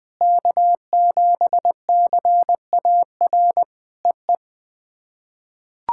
Morse code, heard as a keyed tone, about 20 wpm, 700 Hz. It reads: K7CAR EE
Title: POTA QSO Practice - 20wpm